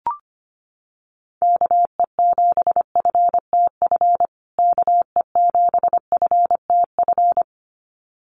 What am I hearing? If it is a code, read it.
Morse code, heard as a keyed tone, about 25 wpm, 700 Hz. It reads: KE7FTF KE7FTF
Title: USA Call Signs Repeated 2x 25wpm